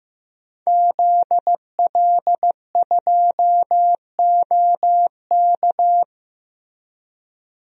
Morse code, heard as a keyed tone, about 15 wpm, 700 Hz. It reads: ZL2OK
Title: DXCC-100 Call Signs - 5 Characters - Repeated 3x - 15wpm